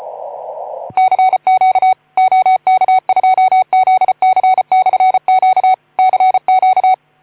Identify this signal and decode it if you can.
Morse code, heard as a keyed tone, about 34 wpm, 750 Hz. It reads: CQ OK2ZC/Q CQ
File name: Archiv / CUC181 / OK2ZC CUC 181